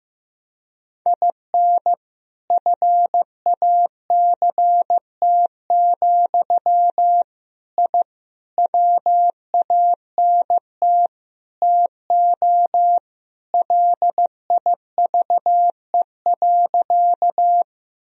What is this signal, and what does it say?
Morse code, heard as a keyed tone, about 15 wpm, 700 Hz. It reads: IN FACT, I WANT TO LIVE.
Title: Sentences from Top 1000 Words - Review Part 3 - 15wpm